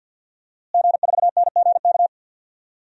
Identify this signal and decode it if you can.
Morse code, heard as a keyed tone, about 46 wpm, 700 Hz. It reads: G4NGK
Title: DXCC-100 Call Signs - 5 Characters - Repeated 3x - 50wpm